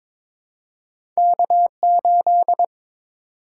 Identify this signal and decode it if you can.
Morse code, heard as a keyed tone, about 22 wpm, 700 Hz. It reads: K8
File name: practice / 22wpm / Two Letter-Number Combinations - 22wpm